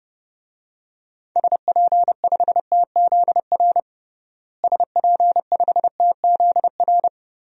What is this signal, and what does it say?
Morse code, heard as a keyed tone, about 30 wpm, 700 Hz. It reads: SP5TZR SP5TZR
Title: DXCC-100 Call Signs - 6 Characters - Repeated 3x - 30wpm